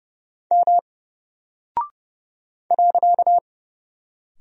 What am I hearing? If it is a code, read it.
Morse code, heard as a keyed tone, about 30 wpm, 700 Hz. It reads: M .
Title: Single Character (Letter:Number:Punctuation:Prosign) Alt 30wpm